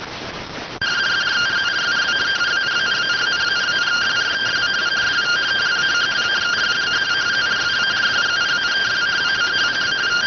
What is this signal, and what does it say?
Two-tone teleprinter signal, about 170 Hz shift.
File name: findings / S ss-portrait.gen-19 rx